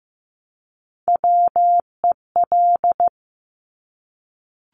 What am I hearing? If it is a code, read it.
Morse code, heard as a keyed tone, about 15 wpm, 700 Hz. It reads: WEL
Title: Top Three Letter Combinations in Words 15wpm